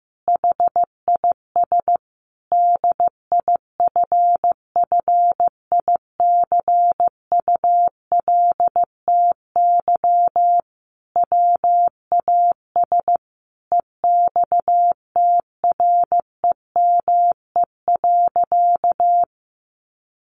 Morse code, heard as a keyed tone, about 15 wpm, 700 Hz. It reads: HIS DIFFICULTY WAS EXTREME.
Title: Sentences from Top 1500 Words 15wpm